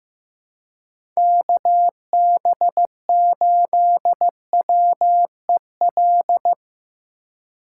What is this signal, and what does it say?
Morse code, heard as a keyed tone, about 15 wpm, 700 Hz. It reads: KB8WEL
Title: USA Call Signs - 2x Character Spacing - 15wpm